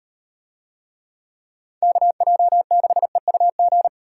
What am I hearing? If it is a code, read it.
Morse code, heard as a keyed tone, about 34 wpm, 700 Hz. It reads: KJ6EUG